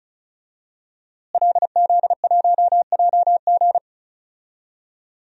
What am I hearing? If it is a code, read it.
Morse code, heard as a keyed tone, about 36 wpm, 700 Hz. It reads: LZ1JG
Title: DXCC-100 Call Signs - 5 Characters - Repeated 2x - 35wpm